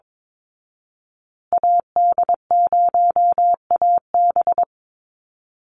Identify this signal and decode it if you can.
Morse code, heard as a keyed tone, about 22 wpm, 700 Hz. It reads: AD0AB
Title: Call Signs Repeated 3x - 22wpm